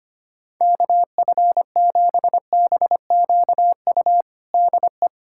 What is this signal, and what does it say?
Morse code, heard as a keyed tone, about 25 wpm, 700 Hz. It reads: KF7BQU DE
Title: USA Callsign DE Callsign 25wpm